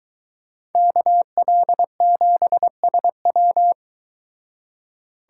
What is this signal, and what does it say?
Morse code, heard as a keyed tone, about 23 wpm, 700 Hz. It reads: KL7SW